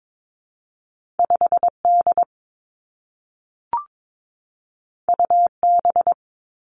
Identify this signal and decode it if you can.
Morse code, heard as a keyed tone, about 22 wpm, 700 Hz. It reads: 5 D U B
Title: Two Challenging Letter-Number Combinations 22wpm